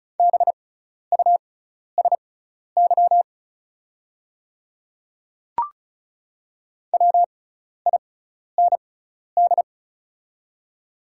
Morse code, heard as a keyed tone, about 36 wpm, spaced as for 12 wpm, 690 Hz. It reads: BUSY WIND
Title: 6x Character Spacing - Top 500 Words - 35wpm